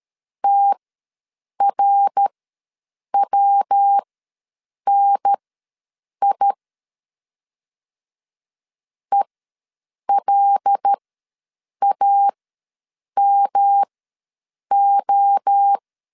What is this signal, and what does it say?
Morse code, heard as a keyed tone, about 13 wpm, spaced as for 7 wpm, 800 Hz. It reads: TRWNI ELAMO
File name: LEZIONI / 6 PKX0 / 6 PKX ws 12wpm 3106